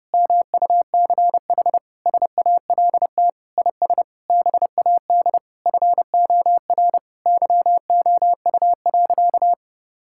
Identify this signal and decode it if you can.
Morse code, heard as a keyed tone, about 30 wpm, 700 Hz. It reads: MUCH SALT IS BAD FOR YOU.
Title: Sentences from Top 1000 Words - Review Part 5 - 30wpm